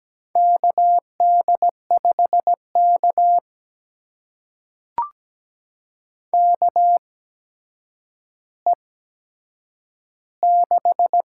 Morse code, heard as a keyed tone, about 17 wpm, spaced as for 8 wpm, 700 Hz. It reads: KD5K KE6
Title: USA Call Signs - 8x Character Spacing - 17wpm